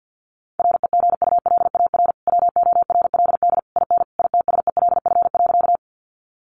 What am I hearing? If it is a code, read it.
Morse code, heard as a keyed tone, about 50 wpm, 700 Hz. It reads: REGULAR WORLD IN ITSELF.
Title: Sentences from Top 900 Words 50wpm